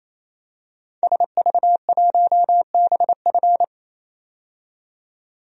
Morse code, heard as a keyed tone, about 27 wpm, 700 Hz. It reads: SV1BF
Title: DXCC-100 Call Signs - 5 Characters - 28wpm